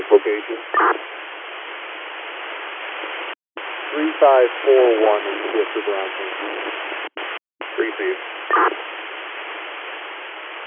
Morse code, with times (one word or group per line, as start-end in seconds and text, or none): none